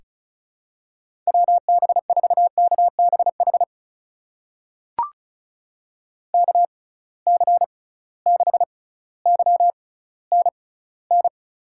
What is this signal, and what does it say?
Morse code, heard as a keyed tone, about 34 wpm, spaced as for 18 wpm, 700 Hz. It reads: WB4KBH KC6YNN
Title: USA Call Signs - 6 Characters - 6x Character Spacing - 35wpm